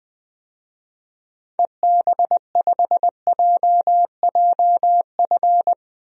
Morse code, heard as a keyed tone, about 20 wpm, 700 Hz. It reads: EB5JJF
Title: DXCC-100 Call Signs - 6 Characters - Repeated 2x - 20wpm